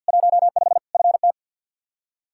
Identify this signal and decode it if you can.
Morse code, heard as a keyed tone, about 47 wpm, 700 Hz. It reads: 15 FT
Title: Practice QSO - Exchange 2 - Info - 50wpm